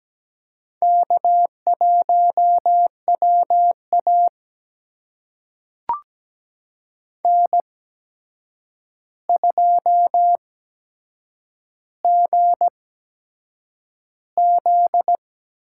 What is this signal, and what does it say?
Morse code, heard as a keyed tone, about 17 wpm, spaced as for 7 wpm, 700 Hz. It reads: K1WA N2GZ
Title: USA Call Signs - 4 Characters - 8x Character Spacing - 17wpm